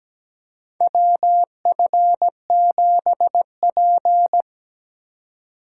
Morse code, heard as a keyed tone, about 17 wpm, 700 Hz. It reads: WF7P
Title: USA Call Signs - 4x Character Spacing - 17wpm